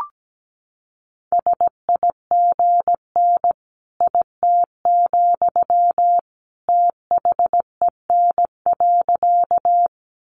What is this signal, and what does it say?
Morse code, heard as a keyed tone, about 17 wpm, 700 Hz. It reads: SIGN IT, THEN.
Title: Sentences from Top 1000 Words 17wpm